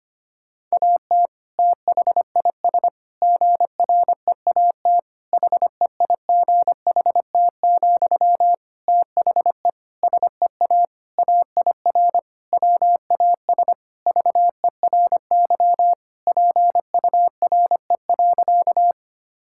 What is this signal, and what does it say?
Morse code, heard as a keyed tone, about 25 wpm, 700 Hz. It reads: AT THIS GREAT HEIGHT, THE SEA AIR WAS VERY PURE.
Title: Sentences from Top 1600 Words 25wpm